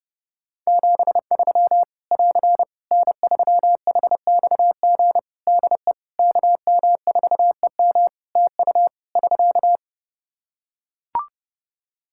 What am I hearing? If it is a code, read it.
Morse code, heard as a keyed tone, about 31 wpm, 700 Hz. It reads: 73 <AR> N3HXG DE KM4EM TU <SK>
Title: Practice QSO - Exchange 4 - Ending - 30wpm